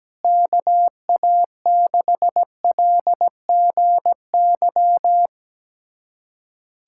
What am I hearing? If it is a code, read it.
Morse code, heard as a keyed tone, about 17 wpm, 700 Hz. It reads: KA6LGY
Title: USA Callsign DE Callsign 17wpm